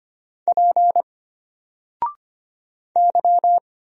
Morse code, heard as a keyed tone, about 25 wpm, 700 Hz. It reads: P Y
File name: Single-Letter-Number-25wpm